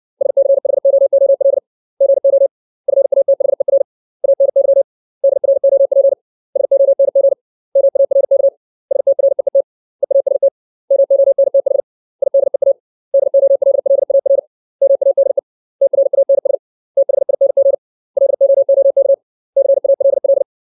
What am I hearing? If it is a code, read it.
Morse code, heard as a keyed tone, about 60 wpm, 550 Hz. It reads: SCHOOL GO FATHER ANY DROP SONG MARK STREET EAST MONTH IDEA DOLLAR MADE TRANS THING LOOK CALL
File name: 632 words / 632 Words 60WPM